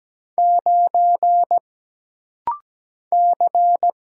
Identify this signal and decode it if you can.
Morse code, heard as a keyed tone, about 17 wpm, 700 Hz. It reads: 9 C
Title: Single Letter-Number Alt - Rapid-Fire 17wpm